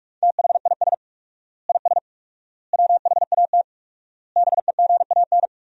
Morse code, heard as a keyed tone, about 46 wpm, 700 Hz. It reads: THIS IS WHAT BEGAN
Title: 4x Word-Spacing - Sentences from Top 300 Words - 45wpm